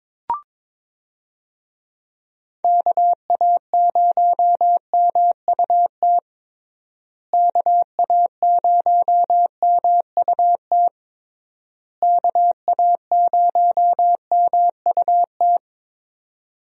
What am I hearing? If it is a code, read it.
Morse code, heard as a keyed tone, about 22 wpm, 700 Hz. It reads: KA0MUT KA0MUT KA0MUT
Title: USA Call Signs - 6 Characters - Repeated 3x - 22wpm